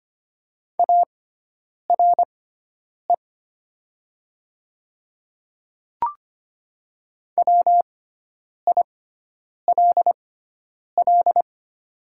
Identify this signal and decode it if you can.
Morse code, heard as a keyed tone, about 25 wpm, spaced as for 9 wpm, 700 Hz. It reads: ARE WILL
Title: 6x Character Spacing - Top 300 Words - 25wpm